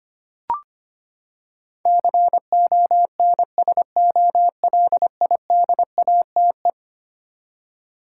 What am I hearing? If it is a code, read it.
Morse code, heard as a keyed tone, about 25 wpm, 700 Hz. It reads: CONSOLIDATE
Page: morse-code-ninja-practice-25wpm